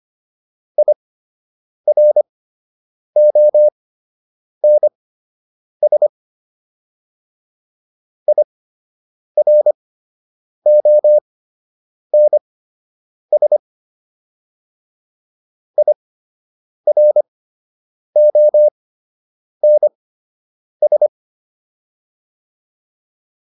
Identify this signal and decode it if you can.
Morse code, heard as a keyed tone, about 25 wpm, spaced as for 8 wpm, 600 Hz. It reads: IRONS IRONS IRONS